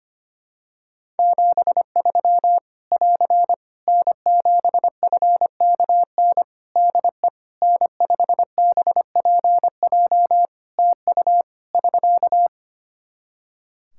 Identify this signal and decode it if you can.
Morse code, heard as a keyed tone, about 25 wpm, 700 Hz. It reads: 73 <AR> N7FKN DE N5BPJ TU <SK>
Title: Practice QSO - Exchange 4 - Ending - 25wpm